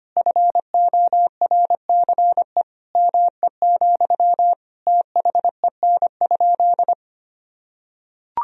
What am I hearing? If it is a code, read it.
Morse code, heard as a keyed tone, about 25 wpm, 700 Hz. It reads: FORCE ME, THEN?
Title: Sentences from Top 1000 Words - Review Part 3 - 25wpm